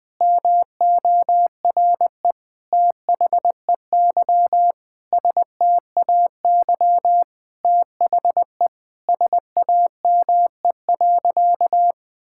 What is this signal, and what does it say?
Morse code, heard as a keyed tone, about 20 wpm, 700 Hz. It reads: MORE THEY STAY THE SAME.
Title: Sentences with 9-Word Phrases - 20wpm